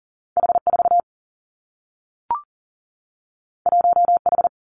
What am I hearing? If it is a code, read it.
Morse code, heard as a keyed tone, about 40 wpm, 700 Hz. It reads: H 4 1 H
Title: Two Challenging Letter-Number Combinations 40wpm